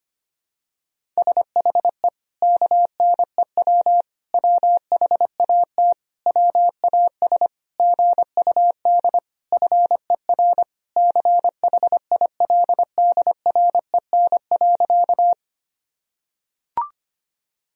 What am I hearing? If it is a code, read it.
Morse code, heard as a keyed tone, about 25 wpm, 700 Hz. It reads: SHE KNEW WHAT WAS GUD FER CHILDREN.
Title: Sentences from Top 400 Words and Abbreviations 25wpm